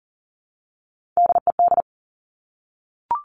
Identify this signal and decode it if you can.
Morse code, heard as a keyed tone, about 40 wpm, 700 Hz. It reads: DED
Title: Top Three Letter Combinations in Words 40wpm